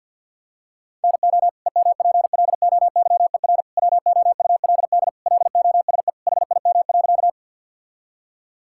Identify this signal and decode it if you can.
Morse code, heard as a keyed tone, about 52 wpm, 700 Hz. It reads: NO EMPLOYER WOULD LOSE HIM.
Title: Sentences from Top 1600 Words 50wpm